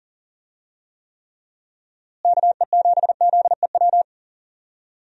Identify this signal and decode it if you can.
Morse code, heard as a keyed tone, about 40 wpm, 700 Hz. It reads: KE7ZEW
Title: USA Call Signs - 6 Characters - Repeated 3x - 40wpm